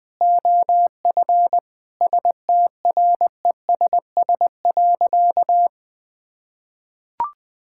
Morse code, heard as a keyed tone, about 20 wpm, 700 Hz. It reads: OF STRESS.
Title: Sentences from Top 1000 Words - Review Part 6 - 20wpm